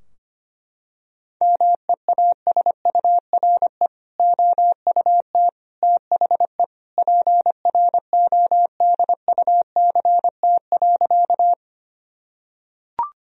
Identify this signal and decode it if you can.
Morse code, heard as a keyed tone, about 25 wpm, 700 Hz. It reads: MEASURE OUT THE PRODUCT.
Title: Sentences from Top 1000 Words - Review Part 2 - 25wpm